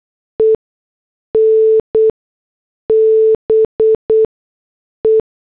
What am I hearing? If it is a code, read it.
Morse code, heard as a keyed tone, about 8 wpm, spaced as for 6 wpm, 440 Hz. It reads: EN BE